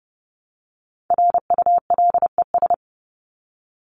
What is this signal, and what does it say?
Morse code, heard as a keyed tone, about 30 wpm, 700 Hz. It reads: RULES
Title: ES:S Suffix - Top 100 Words with Base Words 30wpm